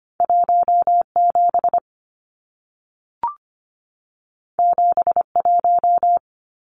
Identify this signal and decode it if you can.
Morse code, heard as a keyed tone, about 25 wpm, 700 Hz. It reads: 17 71
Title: Numbers - 2-Digit 25wpm